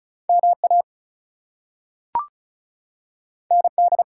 Morse code, heard as a keyed tone, about 36 wpm, 700 Hz. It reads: MA ND
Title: US States and Canadian Provinces 35wpm